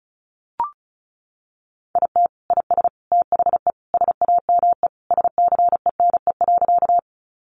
Morse code, heard as a keyed tone, about 34 wpm, 700 Hz. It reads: IT IS THE SAME SCENE.
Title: Sentences from Top 1100 Words 35wpm